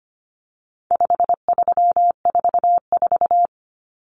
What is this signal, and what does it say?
Morse code, heard as a keyed tone, about 25 wpm, 700 Hz. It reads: 5344
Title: Numbers - 4-Digit 25wpm